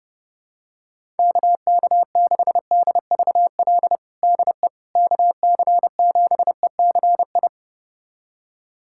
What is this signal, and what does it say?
Morse code, heard as a keyed tone, about 30 wpm, 700 Hz. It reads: KK6DVL DE KC7ECI
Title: USA Callsign DE Callsign - 6 Characters - 30wpm